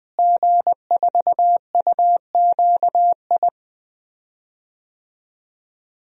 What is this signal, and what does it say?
Morse code, heard as a keyed tone, about 20 wpm, 700 Hz. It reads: G4UQI
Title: DXCC-100 Call Signs - 5 Characters - Repeated 2x - 20wpm